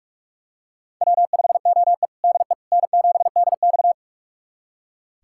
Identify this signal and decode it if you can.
Morse code, heard as a keyed tone, about 46 wpm, 700 Hz. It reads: W5OE DE N7DX
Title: USA Callsign DE Callsign - 4 Characters - 45wpm